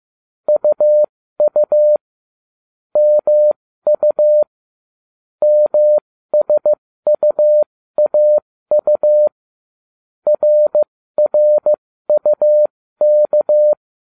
Morse code, heard as a keyed tone, about 15 wpm, spaced as for 13 wpm, 600 Hz. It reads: UU MU MSUAU RRUK